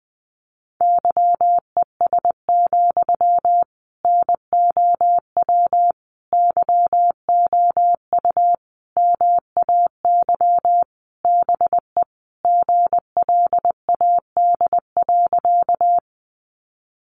Morse code, heard as a keyed tone, about 20 wpm, 700 Hz. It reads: YES, NOW YOU MAY BE GLAD.
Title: Sentences from Top 800 Words 20wpm